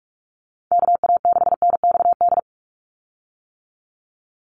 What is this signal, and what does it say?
Morse code, heard as a keyed tone, about 44 wpm, 700 Hz. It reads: KA6NXD
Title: USA Call Signs 45wpm